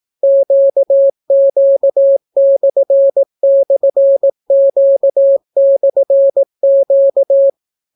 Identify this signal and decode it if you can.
Morse code, heard as a keyed tone, about 18 wpm, 550 Hz. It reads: QQ//Q/Q